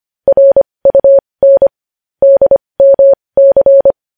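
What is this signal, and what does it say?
Morse code, heard as a keyed tone, about 25 wpm, spaced as for 20 wpm, 550 Hz. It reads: RUN DMC
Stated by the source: Morse code news headlines sent at 20 wpm.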